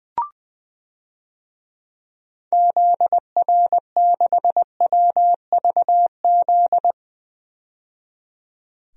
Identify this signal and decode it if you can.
Morse code, heard as a keyed tone, about 20 wpm, 700 Hz. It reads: ZR6WVZ
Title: DXCC-100 Call Signs - 6 Characters - 20wpm